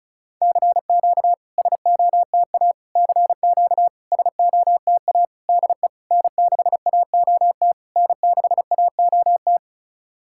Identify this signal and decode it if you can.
Morse code, heard as a keyed tone, about 34 wpm, 690 Hz. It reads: CQ SOTA CQ SOTA DE N6AOT N6AOT
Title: SOTA QSO Practice 35wpm